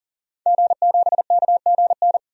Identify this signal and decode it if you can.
Morse code, heard as a keyed tone, about 40 wpm, 700 Hz. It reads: G7KGN
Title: DXCC-100 Call Signs - 5 Characters - 40wpm